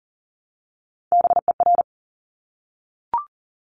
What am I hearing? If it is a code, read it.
Morse code, heard as a keyed tone, about 38 wpm, 700 Hz. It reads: BER
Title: Top Three Letter Combinations in Words 40wpm